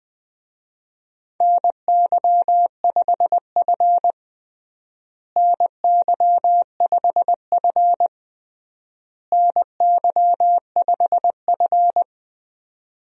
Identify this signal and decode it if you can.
Morse code, heard as a keyed tone, about 20 wpm, 700 Hz. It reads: NY5F NY5F NY5F
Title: USA Call Signs - 4 Characters - Repeated 3x - 20wpm